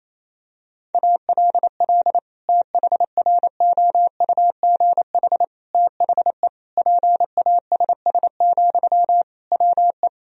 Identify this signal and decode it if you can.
Morse code, heard as a keyed tone, about 28 wpm, 700 Hz. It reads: ALL THROUGH THE PASS, WE
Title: Sentences from Top 1000 Words - Review Part 2 - 28wpm